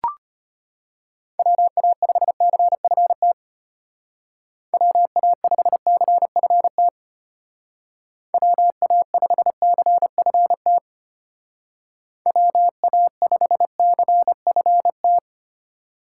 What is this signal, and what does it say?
Morse code, heard as a keyed tone, about 33 wpm, 700 Hz. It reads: WA5CFT WA5CFT WA5CFT WA5CFT